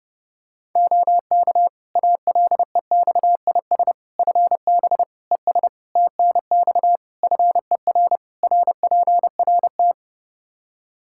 Morse code, heard as a keyed tone, about 32 wpm, 700 Hz. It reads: OK ALEXIS FB ES TNX FER RPRT
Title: Practice QSO - Exchange 2 - Info - 30wpm